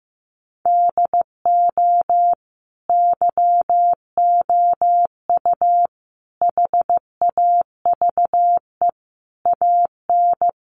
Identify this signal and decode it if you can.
Morse code, heard as a keyed tone, about 15 wpm, 700 Hz. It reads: DO YOU HAVE AN
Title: Sentences from Top 1500 Words 15wpm